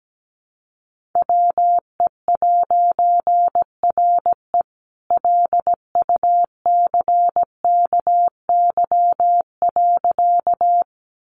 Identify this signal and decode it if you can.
Morse code, heard as a keyed tone, about 17 wpm, 700 Hz. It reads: WE'RE LUCKY.
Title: A Sound Of Thunder 17wpm